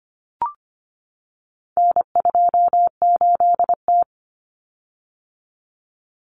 Morse code, heard as a keyed tone, about 25 wpm, 700 Hz. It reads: N28T
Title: Common Cut Numbers - 4-Digit 25wpm